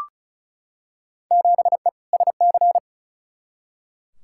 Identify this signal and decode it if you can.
Morse code, heard as a keyed tone, about 36 wpm, 700 Hz. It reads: 7E SC
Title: Field Day Exchanges Encore - 35wpm